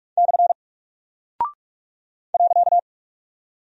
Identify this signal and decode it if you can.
Morse code, heard as a keyed tone, about 45 wpm, 700 Hz. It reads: / .
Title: Single Character (Letter:Number:Punctuation:Prosign) Alt 45wpm